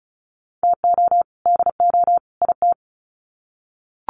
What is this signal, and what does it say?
Morse code, heard as a keyed tone, about 34 wpm, 700 Hz. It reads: TO DO IT
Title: Sets of 3 Words 35wpm